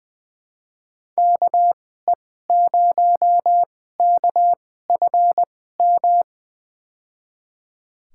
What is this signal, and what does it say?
Morse code, heard as a keyed tone, about 21 wpm, spaced as for 15 wpm, 700 Hz. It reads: KE0KFM
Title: USA Call Signs - 6 Characters - 2x Character Spacing - 20wpm